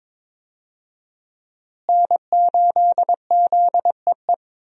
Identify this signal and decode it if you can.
Morse code, heard as a keyed tone, about 22 wpm, 700 Hz. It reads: N8ZEE
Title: USA Call Signs - 5 Characters - Repeated 3x - 22wpm